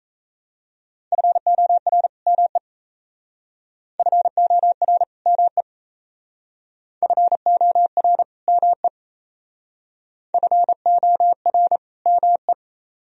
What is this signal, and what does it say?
Morse code, heard as a keyed tone, about 35 wpm, 700 Hz. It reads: FOR ME FOR ME FOR ME FOR ME